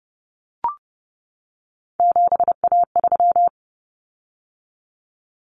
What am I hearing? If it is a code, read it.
Morse code, heard as a keyed tone, about 30 wpm, 700 Hz. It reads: 7A3
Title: Common Cut Numbers - 3-Digit 30wpm